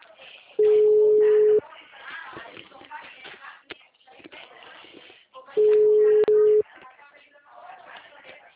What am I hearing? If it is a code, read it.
Ringback tone (Europe, 425 Hz), with speech.
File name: out- -2033- - - .93789